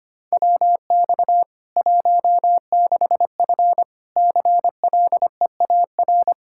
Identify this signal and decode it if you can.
Morse code, heard as a keyed tone, about 25 wpm, 700 Hz. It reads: WX 16F CLEAR
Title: Practice QSO - Exchange 2 - Info - 25wpm